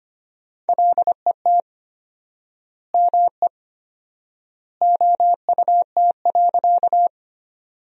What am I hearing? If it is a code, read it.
Morse code, heard as a keyed tone, about 25 wpm, 700 Hz. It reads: LET ME OUT.
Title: 4x Word-Spacing - Sentences from Top 300 Words - 25wpm